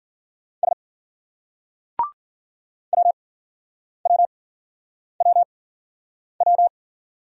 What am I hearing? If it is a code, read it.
Morse code, heard as a keyed tone, about 52 wpm, spaced as for 9 wpm, 700 Hz. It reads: S WWWW